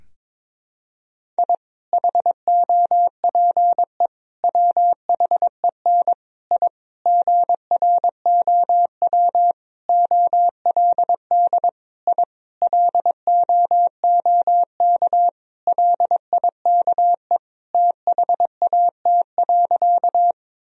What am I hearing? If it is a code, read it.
Morse code, heard as a keyed tone, about 22 wpm, 700 Hz. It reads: I HOPE WHEN I GROW OLD I LOOK LIKE THAT.
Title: Sentences from Top 1000 Words - Review Part 1 - 22wpm